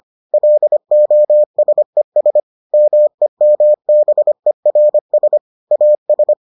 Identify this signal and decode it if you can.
Morse code, heard as a keyed tone, about 25 wpm, 600 Hz. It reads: LOSES MEMBERS AS